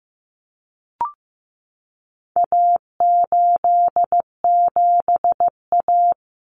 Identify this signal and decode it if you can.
Morse code, heard as a keyed tone, about 15 wpm, 700 Hz. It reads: A87A
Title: Common Cut Numbers - 4-Digit 15wpm